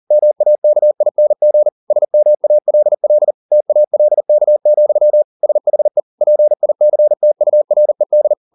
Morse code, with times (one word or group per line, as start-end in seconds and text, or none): 0.1-1.7 MAKING
1.9-3.3 SMALL
3.5-5.2 TALK,
5.4-6.0 SHE
6.2-8.3 PICTURED